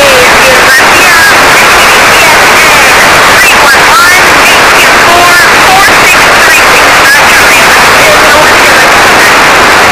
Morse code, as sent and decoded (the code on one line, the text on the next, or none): none